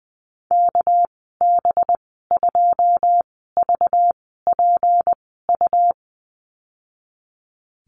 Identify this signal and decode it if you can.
Morse code, heard as a keyed tone, about 20 wpm, spaced as for 15 wpm, 700 Hz. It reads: KB2VPU
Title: 200 Lesson 32g - Letter k - Call Signs 20wpm